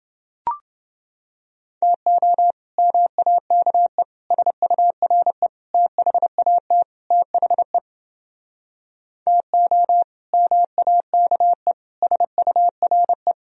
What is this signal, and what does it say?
Morse code, heard as a keyed tone, about 30 wpm, 700 Hz. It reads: TO MAKE SURE THAT THE TO MAKE SURE